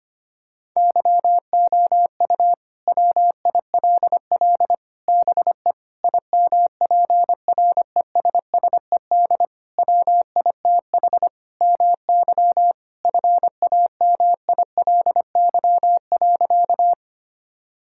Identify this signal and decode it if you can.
Morse code, heard as a keyed tone, about 25 wpm, 700 Hz. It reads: YOU WILL BE IMPRESSED WITH MY FAMILY.
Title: Sentences from Top 1600 Words 25wpm